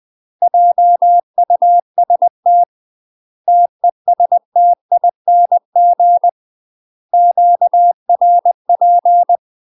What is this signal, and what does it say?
Morse code, heard as a keyed tone, about 20 wpm, 700 Hz. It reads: JUST TESTING QRP
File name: UK MASTER QSOs TRAINING / 20WPM / UK MASTER QSOs 0037